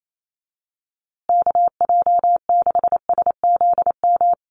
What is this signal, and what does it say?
Morse code, heard as a keyed tone, about 28 wpm, 700 Hz. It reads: KJ6SZM
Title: USA Call Signs Repeated 2x 28wpm